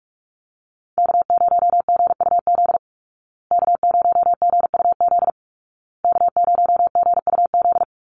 Morse code, heard as a keyed tone, about 44 wpm, 700 Hz. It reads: K0GUZ K0GUZ K0GUZ
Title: Call Signs Repeated 2x - 45wpm